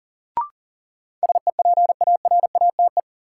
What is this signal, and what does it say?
Morse code, heard as a keyed tone, about 37 wpm, 700 Hz. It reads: SEPARATE
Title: Sentences from Top 1000 Words - Review Part 7 - New Word Drill - 40wpm